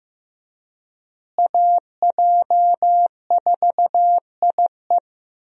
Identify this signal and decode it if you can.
Morse code, heard as a keyed tone, about 16 wpm, 700 Hz. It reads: AJ4IE
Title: USA Call Signs - 5 Characters - Repeated 3x - 15wpm